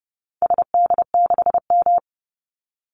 Morse code, heard as a keyed tone, about 30 wpm, 700 Hz. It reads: SD6M
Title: Call Signs - 30wpm